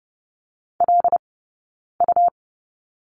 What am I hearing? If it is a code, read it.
Morse code, heard as a keyed tone, about 30 wpm, spaced as for 9 wpm, 690 Hz. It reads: LU
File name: Lesson 17d - Letter u - Two Characters 30wpm